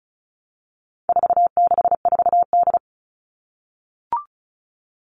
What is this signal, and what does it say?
Morse code, heard as a keyed tone, about 34 wpm, 700 Hz. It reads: 464D
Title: All Cut Numbers - 4-Digit 35wpm